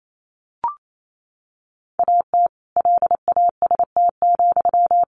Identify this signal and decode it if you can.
Morse code, heard as a keyed tone, about 28 wpm, 700 Hz. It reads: AT LAST,
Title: Sentences from Top 1400 Words 28wpm